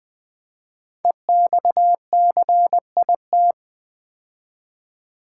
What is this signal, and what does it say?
Morse code, heard as a keyed tone, about 21 wpm, 700 Hz. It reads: EXCIT
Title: Build A Word - 5 Letters At A Time - 10-Letter Words - 20wpm